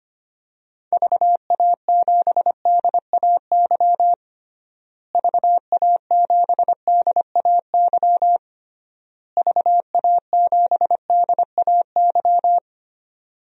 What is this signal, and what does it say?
Morse code, heard as a keyed tone, about 25 wpm, 700 Hz. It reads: VA7DAY VA7DAY VA7DAY
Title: DXCC-100 Call Signs - 6 Characters - Repeated 3x - 25wpm